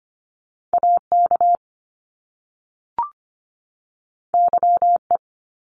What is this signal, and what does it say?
Morse code, heard as a keyed tone, about 25 wpm, 700 Hz. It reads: AK YE
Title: Top Two Letter Combinations in Words 25wpm